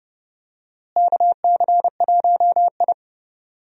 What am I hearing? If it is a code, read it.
Morse code, heard as a keyed tone, about 30 wpm, 700 Hz. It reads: KC1I
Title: USA Call Signs - 4 Characters - Repeated 3x - 30wpm